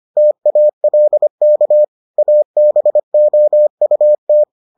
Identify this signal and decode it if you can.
Morse code, heard as a keyed tone, about 25 wpm, 600 Hz. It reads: TALK ABOUT